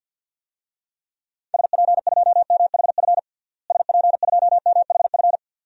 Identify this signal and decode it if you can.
Morse code, heard as a keyed tone, about 52 wpm, 700 Hz. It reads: SP2MHF SP2MHF
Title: DXCC-100 Call Signs - 6 Characters - Repeated 2x - 50wpm